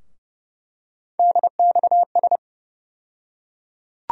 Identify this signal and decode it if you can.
Morse code, heard as a keyed tone, about 30 wpm, 700 Hz. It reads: DXS
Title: USA Call Sign Postfixes - 3 Letters 30wpm